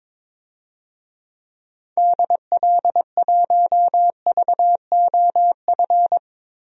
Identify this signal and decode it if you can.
Morse code, heard as a keyed tone, about 21 wpm, 700 Hz. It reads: DL1VOF
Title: DXCC-100 Call Signs - 6 Characters - 22wpm